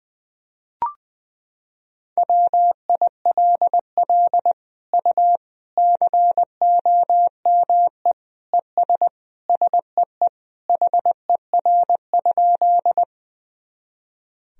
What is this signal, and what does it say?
Morse code, heard as a keyed tone, about 21 wpm, 700 Hz. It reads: WILL U COME ES SEE HER?
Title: Sentences from Top 100 Words and Abbreviations 20wpm